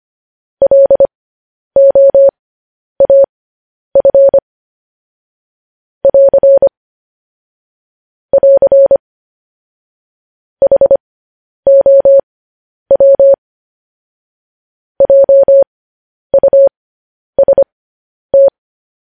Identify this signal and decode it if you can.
Morse code, heard as a keyed tone, about 25 wpm, spaced as for 10 wpm, 550 Hz. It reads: LOAF <AR> <AR> HOW JUST
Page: news headlines in morse code at 10 wpm